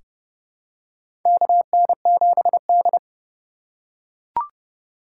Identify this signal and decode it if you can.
Morse code, heard as a keyed tone, about 32 wpm, 700 Hz. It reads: KN7D
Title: K1USN SST QSO Practice - 30wpm